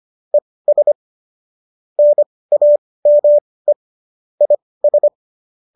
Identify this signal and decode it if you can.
Morse code, heard as a keyed tone, about 25 wpm, spaced as for 18 wpm, 590 Hz. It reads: ES NAME IS
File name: qso207 18